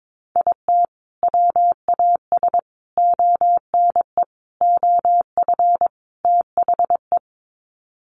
Morse code, heard as a keyed tone, about 22 wpm, 700 Hz. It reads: IT WAS ONE OF THE
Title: Sets of 5 Words 22wpm